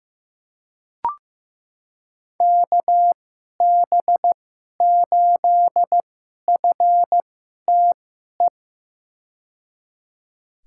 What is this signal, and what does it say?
Morse code, heard as a keyed tone, about 15 wpm, spaced as for 11 wpm, 700 Hz. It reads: KB8FTE
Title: USA Call Signs - 6 Characters - 2x Character Spacing - 15wpm